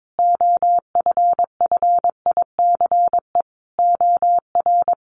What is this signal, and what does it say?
Morse code, heard as a keyed tone, about 22 wpm, 700 Hz. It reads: OFFICE OR
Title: Sentences from Top 1000 Words 22wpm